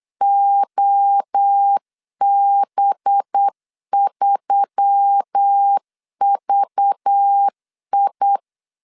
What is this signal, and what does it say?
Morse code, heard as a keyed tone, about 8 wpm, 800 Hz. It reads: OB3VI
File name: LEZIONI / 6 PKX0 / 6 PK 8wpm 3003